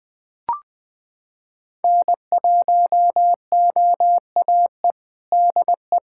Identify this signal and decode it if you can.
Morse code, heard as a keyed tone, about 20 wpm, 700 Hz. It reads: N1OAE DE
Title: USA Callsign DE Callsign - 5 Characters - 20wpm